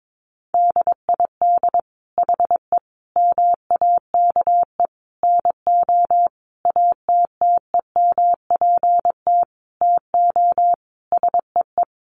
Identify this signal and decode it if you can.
Morse code, heard as a keyed tone, about 22 wpm, 700 Hz. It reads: DID HE MAKE NO ATTEMPT TO SEE
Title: Sentences from Top 1000 Words 22wpm